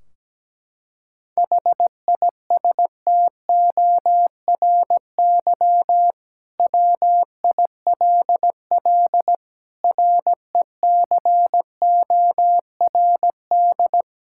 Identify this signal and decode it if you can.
Morse code, heard as a keyed tone, about 17 wpm, 700 Hz. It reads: HISTORY WILL RECORD
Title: Sentences from Top 1600 Words 17wpm